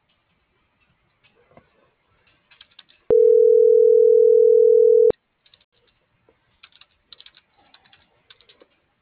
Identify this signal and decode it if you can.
Ringback tone (North America), 440 + 480 Hz.